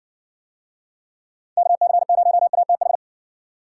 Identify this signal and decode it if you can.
Morse code, heard as a keyed tone, about 60 wpm, 700 Hz. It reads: KC0ATH